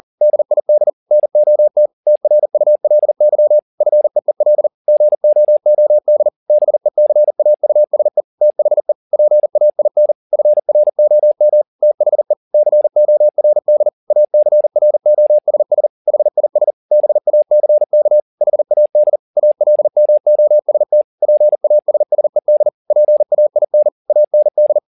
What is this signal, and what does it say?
Morse code, heard as a keyed tone, about 40 wpm, 600 Hz. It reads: DID NOT TRULY FEEL GOOD BECAUSE THE PAIN FROM THE CORD ACROSS HIS BACK HAD ALMOST PASSED PAIN AND
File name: the old man and the sea 2 40wpm